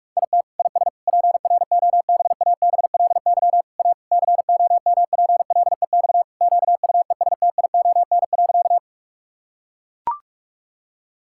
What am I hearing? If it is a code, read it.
Morse code, heard as a keyed tone, about 41 wpm, 700 Hz. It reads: IT IS PROBABLY A COMPLEX QUESTION.
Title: Sentences from Top 1000 Words - Review Part 6 - 45wpm